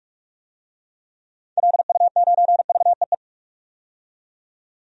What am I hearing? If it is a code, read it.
Morse code, heard as a keyed tone, about 41 wpm, 700 Hz. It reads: LU9VEE
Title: DXCC-100 Call Signs - 6 Characters - 45wpm